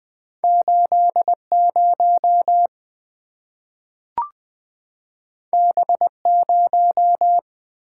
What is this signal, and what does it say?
Morse code, heard as a keyed tone, about 20 wpm, 700 Hz. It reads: 8 0 B 0
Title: Two Letter-Number Combinations - 20wpm